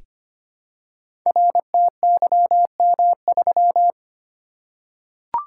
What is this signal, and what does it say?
Morse code, heard as a keyed tone, about 26 wpm, 700 Hz. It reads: RTYM3
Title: Code Groups - 5 Letter:Numbers - 25wpm